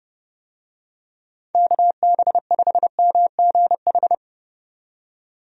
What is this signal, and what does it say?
Morse code, heard as a keyed tone, about 30 wpm, 700 Hz. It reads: KB5MGH
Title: USA Call Signs - 6 Characters - 30wpm